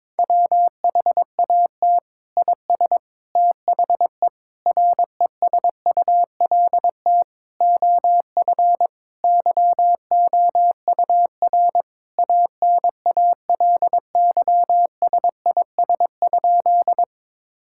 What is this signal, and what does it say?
Morse code, heard as a keyed tone, about 22 wpm, 700 Hz. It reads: WHAT IS THE RESULT OF YOUR ANALYSIS?
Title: Sentences from Top 1000 Words - Review Part 5 - 22wpm